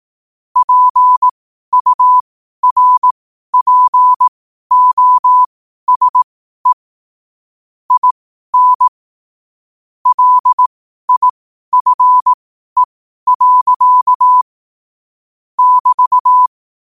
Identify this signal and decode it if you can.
Morse code, heard as a keyed tone, about 18 wpm, spaced as for 12 wpm, 1000 Hz. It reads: PURPOSE IN LIFE. <BT>
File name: qotdfiles / 12wpm